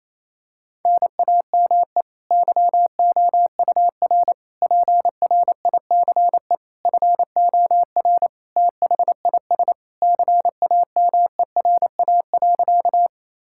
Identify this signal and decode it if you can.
Morse code, heard as a keyed tone, about 28 wpm, 700 Hz. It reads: NAME YOUR PRICE FOR THIS CAMERA.
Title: Sentences from Top 1000 Words - Review Part 5 - 28wpm